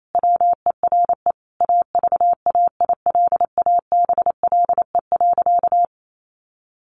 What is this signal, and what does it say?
Morse code, heard as a keyed tone, about 28 wpm, 700 Hz. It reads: WERE AVAILABLE.
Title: Sentences from Top 1100 Words 28wpm